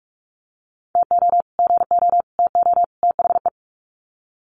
Morse code, heard as a keyed tone, about 44 wpm, 700 Hz. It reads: TO GO TO THE
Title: Sets of 4 Words 45wpm